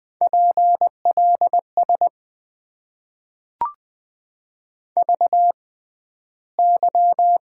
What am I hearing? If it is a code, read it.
Morse code, heard as a keyed tone, about 20 wpm, spaced as for 12 wpm, 700 Hz. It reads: PLS VY
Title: Top 90 QSO Elements - 6x Character Spacing - 20wpm